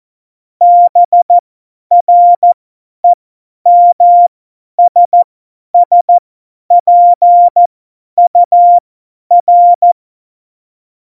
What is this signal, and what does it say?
Morse code, heard as a keyed tone, about 14 wpm, spaced as for 10 wpm, 700 Hz. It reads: BREMSSPUR
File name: Morsekurs / Witze / Witz 24. September 2019, 50 F70